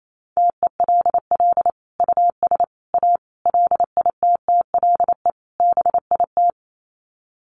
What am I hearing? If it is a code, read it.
Morse code, heard as a keyed tone, about 28 wpm, 700 Hz. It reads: TELL US A LITTLE BIT
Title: Sets of 5 Words Encore2 28wpm